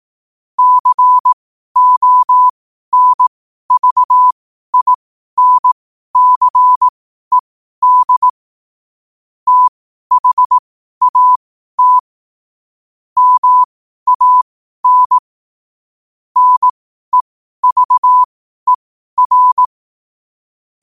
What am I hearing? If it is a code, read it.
Morse code, heard as a keyed tone, about 18 wpm, spaced as for 12 wpm, 1000 Hz. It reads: CONVINCED THAT MAN NEVER
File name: qotdfiles / 12wpm